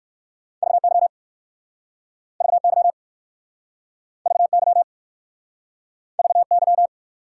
Morse code, heard as a keyed tone, about 59 wpm, 700 Hz. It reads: VY VY VY VY